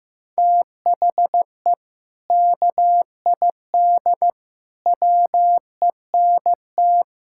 Morse code, heard as a keyed tone, about 15 wpm, 700 Hz. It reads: THE KID WENT
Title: Sentences from Top 1600 Words 15wpm